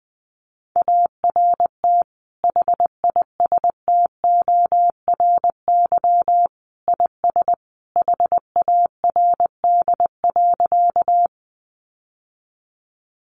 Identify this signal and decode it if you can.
Morse code, heard as a keyed tone, about 19 wpm, 700 Hz. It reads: ART HISTORY IS HARD.
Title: Sentences from Top 600 Words 20wpm